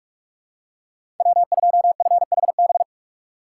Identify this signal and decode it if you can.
Morse code, heard as a keyed tone, about 43 wpm, 700 Hz. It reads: W2FHB
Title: USA Call Signs - 5 Characters - 3x Character Spacing - 45wpm